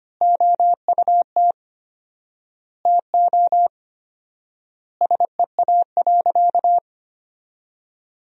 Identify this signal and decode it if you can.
Morse code, heard as a keyed tone, about 25 wpm, 700 Hz. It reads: OUT TO SEA.
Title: 4x Word-Spacing - Sentences from Top 400 Words - 25wpm